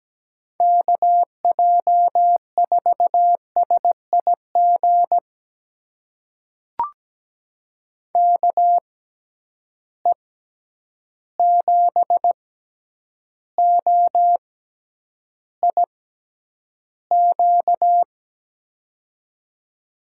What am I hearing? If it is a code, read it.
Morse code, heard as a keyed tone, about 17 wpm, spaced as for 9 wpm, 700 Hz. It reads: KJ4SIG KE7OIQ
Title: USA Call Signs - 6 Characters - 6x Character Spacing - 17wpm